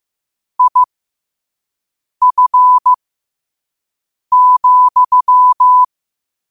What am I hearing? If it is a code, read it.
Morse code, heard as a keyed tone, about 15 wpm, spaced as for 5 wpm, 1000 Hz. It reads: IF,
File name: qotdfiles / 5wpm